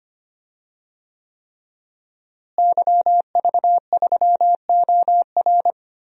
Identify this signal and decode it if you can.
Morse code, heard as a keyed tone, about 25 wpm, 700 Hz. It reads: YV3OR
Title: DXCC-100 Call Signs - 5 Characters - Repeated 3x - 25wpm